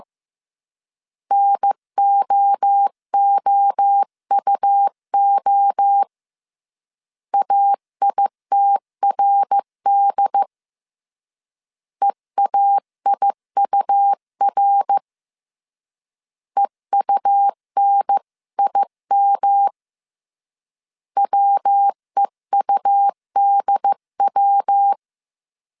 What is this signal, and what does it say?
Morse code, heard as a keyed tone, about 15 wpm, 800 Hz. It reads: NOOUO AITRD EAIUR EUNIM WEUDW
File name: CW / DET / 1 3 1500C S C 14wpm 2458